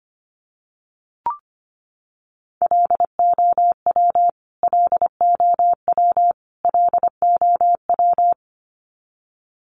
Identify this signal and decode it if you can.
Morse code, heard as a keyed tone, about 24 wpm, 700 Hz. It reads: LOW LOW LOW
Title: Top 200 Words Repeated 25wpm